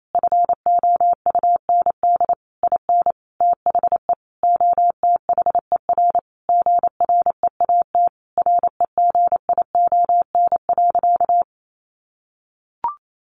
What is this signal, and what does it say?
Morse code, heard as a keyed tone, about 28 wpm, 700 Hz. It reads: FOUND IN THE OTHER GREAT REGION.
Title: Sentences from Top 1500 Words 28wpm